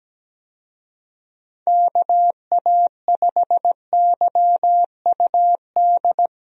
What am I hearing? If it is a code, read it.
Morse code, heard as a keyed tone, about 17 wpm, 700 Hz. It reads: KA5YUD
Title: USA Call Signs - 6 Characters - Repeated 2x - 17wpm